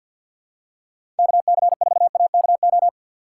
Morse code, heard as a keyed tone, about 48 wpm, 700 Hz. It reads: KC4AKO
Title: USA Call Signs - 6 Characters - 3x Character Spacing - 50wpm